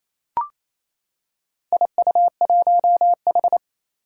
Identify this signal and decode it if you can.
Morse code, heard as a keyed tone, about 29 wpm, 700 Hz. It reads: IU1H
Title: Four Challenging Letter-Number Combinations 28wpm